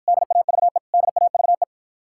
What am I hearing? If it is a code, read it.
Morse code, heard as a keyed tone, about 48 wpm, 700 Hz. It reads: DAVE DAVE